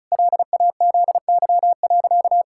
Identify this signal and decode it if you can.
Morse code, heard as a keyed tone, about 36 wpm, 700 Hz. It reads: LAZY.
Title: Pangram Challenge v2 - 35wpm